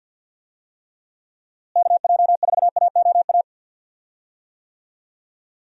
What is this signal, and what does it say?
Morse code, heard as a keyed tone, about 51 wpm, 700 Hz. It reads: KJ4AOA
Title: USA Call Signs - 6 Characters - Repeated 2x - 50wpm